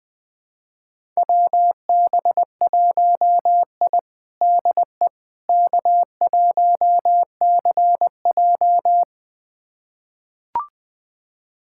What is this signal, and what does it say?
Morse code, heard as a keyed tone, about 21 wpm, 700 Hz. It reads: WB1I DE K1CJ
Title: USA Callsign DE Callsign - 4 Characters - 20wpm